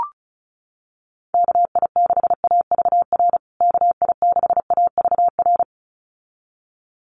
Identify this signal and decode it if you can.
Morse code, heard as a keyed tone, about 34 wpm, 700 Hz. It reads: KI6AVR KI6AVR
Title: USA Call Signs Repeated 2x 35wpm